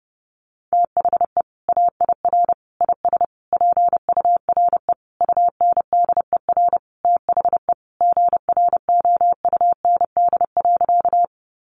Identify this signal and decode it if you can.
Morse code, heard as a keyed tone, about 30 wpm, 700 Hz. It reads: THE AIR IS PURE UNDER THE GROUND.
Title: Sentences from Top 1500 Words 30wpm